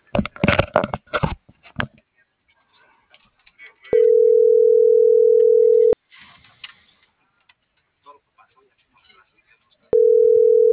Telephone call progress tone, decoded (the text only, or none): ringback tone